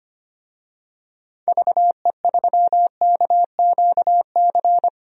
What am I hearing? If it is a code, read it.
Morse code, heard as a keyed tone, about 25 wpm, 700 Hz. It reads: VE3KQC
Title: DXCC-100 Call Signs - 6 Characters - 25wpm